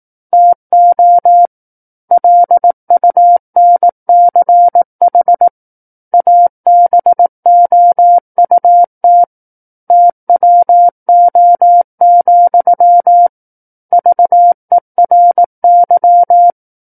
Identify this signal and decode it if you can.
Morse code, heard as a keyed tone, about 18 wpm, 700 Hz. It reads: TO LUNCH ABOUT TWO, VERY
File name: 18 WPM